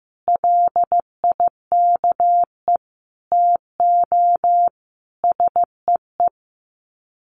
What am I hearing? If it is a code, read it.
Morse code, heard as a keyed tone, about 15 wpm, 700 Hz. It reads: LIKE TO SEE
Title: Sets of 4 Words 15wpm